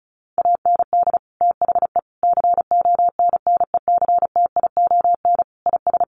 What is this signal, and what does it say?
Morse code, heard as a keyed tone, about 34 wpm, 700 Hz. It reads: AND THE CONNECTION IS
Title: Sentences from Top 1300 Words 35wpm